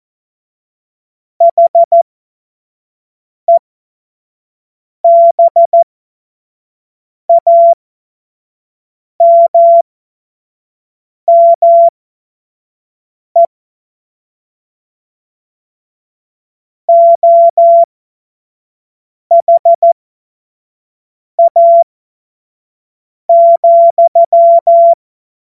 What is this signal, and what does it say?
Morse code, heard as a keyed tone, about 14 wpm, spaced as for 5 wpm, 680 Hz. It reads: HEBAMME OHA,
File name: Morsekurs / Witze / Witz 06. Januar 2026 25 F70